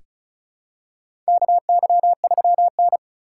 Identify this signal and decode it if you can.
Morse code, heard as a keyed tone, about 36 wpm, 700 Hz. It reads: KY3N
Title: USA Call Signs - 4 Characters - 3x Character Spacing - 35wpm